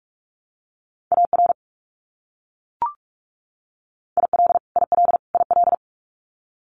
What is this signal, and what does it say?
Morse code, heard as a keyed tone, about 44 wpm, 700 Hz. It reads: AR IL IL IL
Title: US States Repeated 3x 45wpm